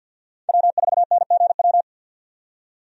Morse code, heard as a keyed tone, about 50 wpm, 700 Hz. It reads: W4NGW
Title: USA Call Signs - 5 Characters - 50wpm